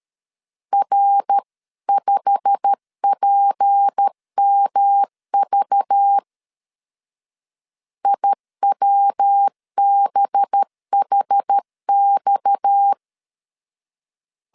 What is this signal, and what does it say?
Morse code, heard as a keyed tone, about 13 wpm, 800 Hz. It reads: R5PMV IWBHX
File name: CW / DET / 1 9 1500C S A 12wpm 3800